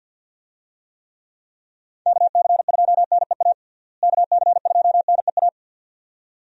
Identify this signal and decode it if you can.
Morse code, heard as a keyed tone, about 47 wpm, 700 Hz. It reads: KC2NEA KC2NEA
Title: USA Call Signs - 6 Characters - Repeated 2x - 50wpm